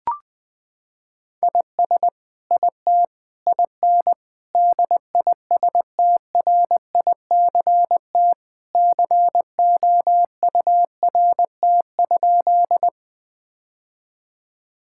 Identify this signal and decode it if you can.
Morse code, heard as a keyed tone, about 20 wpm, 700 Hz. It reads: IS IT IN DISTRICT COURT?
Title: Sentences from Top 1600 Words 20wpm